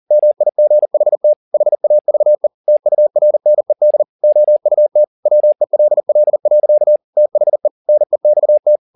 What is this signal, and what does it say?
Morse code, heard as a keyed tone, about 40 wpm, 600 Hz. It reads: MIGHT HAVE TURNED OUT WELL. THE NEXT
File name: the old man and the sea 2 40wpm